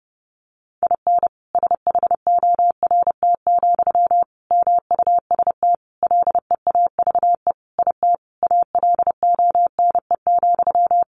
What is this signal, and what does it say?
Morse code, heard as a keyed tone, about 30 wpm, 700 Hz. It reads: IN SHORT, MUST LEAVE IT ALONE,
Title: Sentences from Top 1000 Words 30wpm